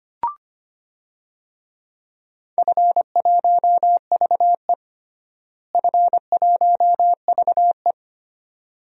Morse code, heard as a keyed tone, about 25 wpm, 700 Hz. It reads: F1VE F1VE
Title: DXCC-100 Call Signs - 4 Characters - Repeated 2x - 25wpm